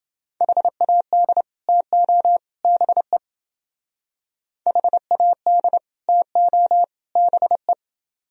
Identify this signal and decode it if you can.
Morse code, heard as a keyed tone, about 30 wpm, 700 Hz. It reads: HAD TO BE HAD TO BE